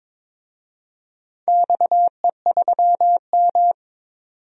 Morse code, heard as a keyed tone, about 22 wpm, 700 Hz. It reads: XE3M
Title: DXCC-100 Call Signs - 4 Characters - 22wpm